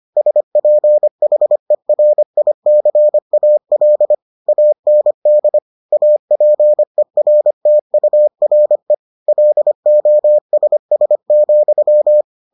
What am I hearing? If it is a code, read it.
Morse code, heard as a keyed tone, about 25 wpm, 600 Hz. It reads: SPHERICAL AND APERTURE LOSS,